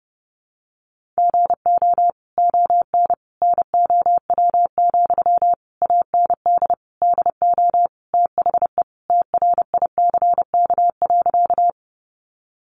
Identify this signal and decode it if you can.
Morse code, heard as a keyed tone, about 30 wpm, 700 Hz. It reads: GO ON NOW, AND DO THE TRICK.
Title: Sentences from Top 1500 Words 30wpm